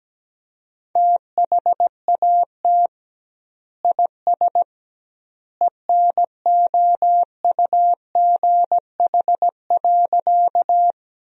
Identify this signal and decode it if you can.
Morse code, heard as a keyed tone, about 17 wpm, 700 Hz. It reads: THAT IS ENOUGH.
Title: 2x Word-Spacing - Sentences from Top 400 Words - 17wpm